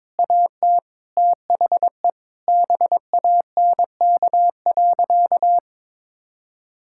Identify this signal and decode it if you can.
Morse code, heard as a keyed tone, about 22 wpm, 700 Hz. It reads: AT THE BANK.
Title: Sentences from Top 1000 Words - Review Part 6 - 22wpm